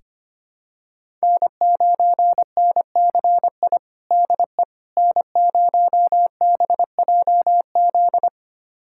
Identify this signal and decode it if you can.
Morse code, heard as a keyed tone, about 25 wpm, 700 Hz. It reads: N9NCI DE N0BJZ
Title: USA Callsign DE Callsign - 5 Characters - 25wpm